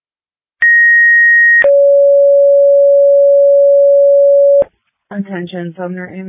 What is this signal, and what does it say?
Two-tone page, heard as a one-second tone, then a three-second tone, about 1900 then 560 Hz, with speech. It is heard from 0.6 to 4.6 s.